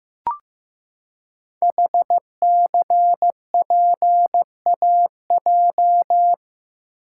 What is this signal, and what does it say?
Morse code, heard as a keyed tone, about 15 wpm, 700 Hz. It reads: HCPAJ
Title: Code Groups - 5 Letters - 15wpm